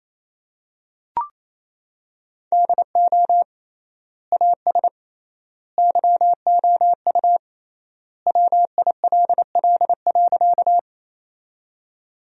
Morse code, heard as a keyed tone, about 28 wpm, 700 Hz. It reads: DO AS YOU WILL.
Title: 3x Word-Spacing - Sentences from Top 100 Words - 28wpm